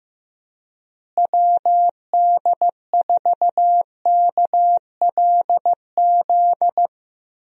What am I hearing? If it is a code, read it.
Morse code, heard as a keyed tone, about 15 wpm, 700 Hz. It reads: WD4KLZ
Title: USA Call Signs - 6 Characters - Repeated 3x - 15wpm